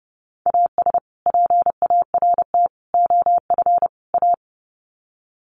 Morse code, heard as a keyed tone, about 29 wpm, 700 Hz. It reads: AS PART OF A
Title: Sets of 4 Words 30wpm